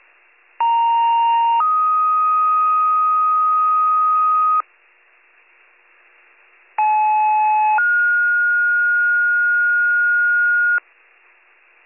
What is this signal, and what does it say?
Two-tone page, heard as a one-second tone, then a three-second tone, about 930 then 1250 Hz, about 860 then 1430 Hz. It is heard from 0.6 to 10.8 s.